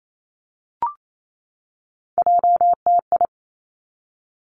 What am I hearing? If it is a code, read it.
Morse code, heard as a keyed tone, about 28 wpm, 700 Hz. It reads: JTI
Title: Three Challenging Letter-Number Combinations 28wpm